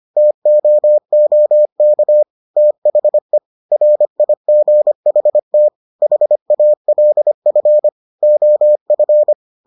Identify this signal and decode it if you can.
Morse code, heard as a keyed tone, about 25 wpm, 600 Hz. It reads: TOOK THE RIGHT HALF OF